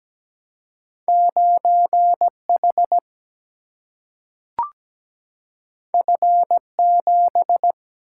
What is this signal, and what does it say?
Morse code, heard as a keyed tone, about 17 wpm, 700 Hz. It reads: 9 H F 7
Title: Single-Letter Single-Number 17wpm